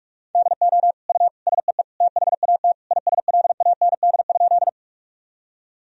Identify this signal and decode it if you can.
Morse code, heard as a keyed tone, about 41 wpm, 700 Hz. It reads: DO U SEE THAT ISLAND?
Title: Sentences from Top 500 Words and Abbreviations 45wpm